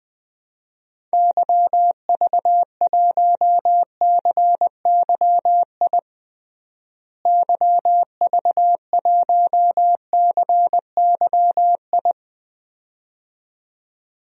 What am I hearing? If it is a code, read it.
Morse code, heard as a keyed tone, about 20 wpm, 700 Hz. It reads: YV1CYI YV1CYI
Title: DXCC-100 Call Signs - 6 Characters - Repeated 2x - 20wpm